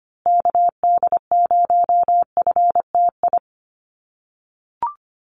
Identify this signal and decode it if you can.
Morse code, heard as a keyed tone, about 25 wpm, 700 Hz. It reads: KD0FTI
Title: USA Call Signs Repeated 3x 25wpm